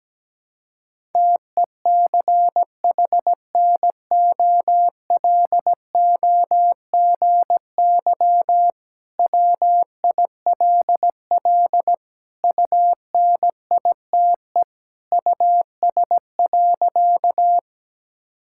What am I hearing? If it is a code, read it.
Morse code, heard as a keyed tone, about 17 wpm, 700 Hz. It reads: TECHNOLOGY WILL UNITE US.
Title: Sentences from Top 1000 Words - Review Part 5 - 17wpm